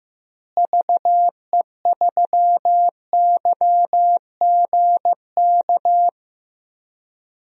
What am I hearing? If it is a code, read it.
Morse code, heard as a keyed tone, about 15 wpm, 700 Hz. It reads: VE3YGK
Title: DXCC-100 Call Signs - 6 Characters - Repeated 3x - 15wpm